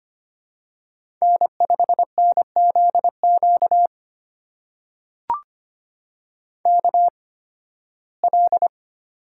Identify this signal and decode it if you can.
Morse code, heard as a keyed tone, about 25 wpm, spaced as for 16 wpm, 700 Hz. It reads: N5NZQ KL
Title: USA Call Signs - 5 Characters - 8x Character Spacing - 25wpm